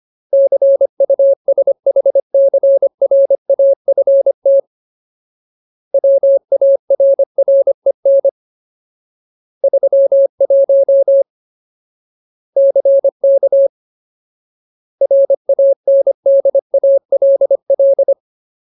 Morse code, heard as a keyed tone, about 25 wpm, 550 Hz. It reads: CUSHCRAFT WARREN 31 CK RANDALL
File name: Ultimate Word List / 25WPM Ultimate